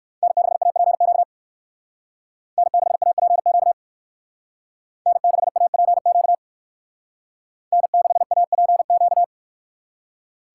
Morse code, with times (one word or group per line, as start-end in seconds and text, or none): 0.2-1.2 N6APQ
2.6-3.7 N6APQ
5.1-6.3 N6APQ
7.7-9.2 N6APQ